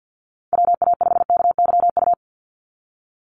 Morse code, heard as a keyed tone, about 48 wpm, 700 Hz. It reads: WA5KYU
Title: USA Call Signs 50wpm